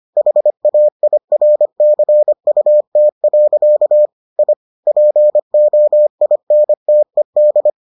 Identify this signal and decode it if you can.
Morse code, heard as a keyed tone, about 25 wpm, 600 Hz. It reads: HAIRCUT. I POINTED